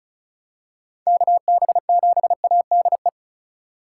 Morse code, heard as a keyed tone, about 34 wpm, 700 Hz. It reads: KB7ADE
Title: USA Call Signs - 6 Characters - 4x Character Spacing - 35wpm